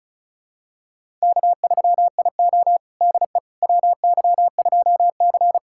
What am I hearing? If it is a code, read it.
Morse code, heard as a keyed tone, about 36 wpm, 700 Hz. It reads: K3IO DE WY2C
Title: USA Callsign DE Callsign - 4 Characters - 35wpm